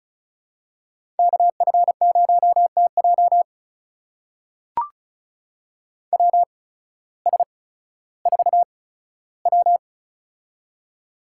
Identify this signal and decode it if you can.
Morse code, heard as a keyed tone, about 36 wpm, spaced as for 16 wpm, 700 Hz. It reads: KF0TJ WS4W
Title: USA Call Signs - 8x Character Spacing - 35wpm